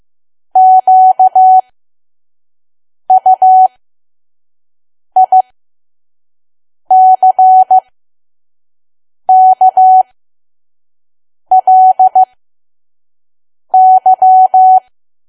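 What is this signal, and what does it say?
Morse code, heard as a keyed tone, about 15 wpm, spaced as for 5 wpm, 750 Hz. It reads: QUICKLY
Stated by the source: Morse code practice at 5 WPM from KQ2P.COM.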